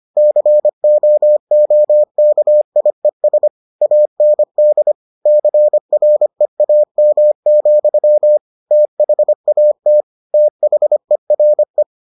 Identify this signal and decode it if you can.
Morse code, heard as a keyed tone, about 25 wpm, 600 Hz. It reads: COOKIES AND CREAM, THAT THERE